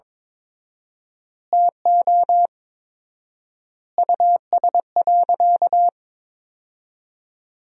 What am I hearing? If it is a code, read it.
Morse code, heard as a keyed tone, about 23 wpm, 700 Hz. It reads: TO US.
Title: 4x Word-Spacing - Sentences from Top 200 Words - 22wpm